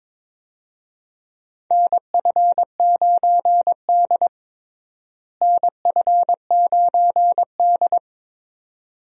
Morse code, heard as a keyed tone, about 22 wpm, 700 Hz. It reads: NF9D NF9D
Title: USA Call Signs - 4 Characters - Repeated 2x - 22wpm